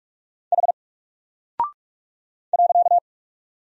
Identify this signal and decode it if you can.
Morse code, heard as a keyed tone, about 44 wpm, 700 Hz. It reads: H .
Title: Single Character (Letter:Number:Punctuation:Prosign) 45wpm